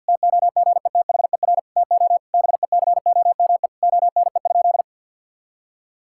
Morse code, heard as a keyed tone, about 50 wpm, 700 Hz. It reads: TOGETHER TO BECOME ONE?
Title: Sentences from Top 1600 Words 50wpm